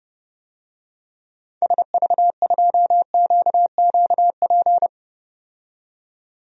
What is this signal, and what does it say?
Morse code, heard as a keyed tone, about 30 wpm, 700 Hz. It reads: SV2QQP
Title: DXCC-100 Call Signs - 6 Characters - 30wpm